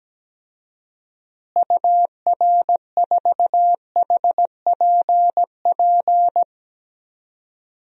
Morse code, heard as a keyed tone, about 17 wpm, 700 Hz. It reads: UR4HPP
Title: DXCC-100 Call Signs - 6 Characters - Repeated 3x - 17wpm